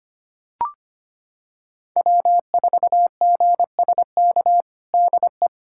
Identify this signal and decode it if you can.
Morse code, heard as a keyed tone, about 25 wpm, 700 Hz. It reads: W4GSK DE
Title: USA Callsign DE Callsign 25wpm